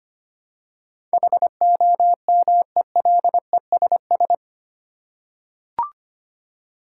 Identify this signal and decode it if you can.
Morse code, heard as a keyed tone, about 25 wpm, 700 Hz. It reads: HOMELESS
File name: Eight Letter Words - 25wpm